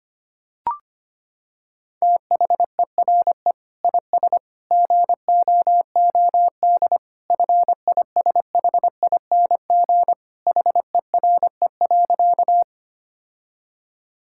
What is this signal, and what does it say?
Morse code, heard as a keyed tone, about 25 wpm, 700 Hz. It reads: THERE IS GOOD FISHING HERE.
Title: Sentences from Top 1000 Words - Review Part 5 - 25wpm